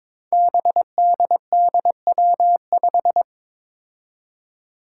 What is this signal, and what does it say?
Morse code, heard as a keyed tone, about 22 wpm, 700 Hz. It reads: BDDW5
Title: Five Challenging Letter-Number Combinations 22wpm